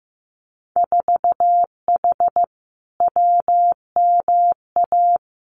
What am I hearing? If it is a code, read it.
Morse code, heard as a keyed tone, about 15 wpm, 700 Hz. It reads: 4H WMA
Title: Winter Field Day Exchanges Encore 15wpm